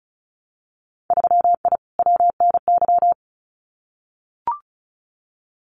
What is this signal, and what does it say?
Morse code, heard as a keyed tone, about 34 wpm, 700 Hz. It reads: 3I WNY
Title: Winter Field Day Exchanges Encore 35wpm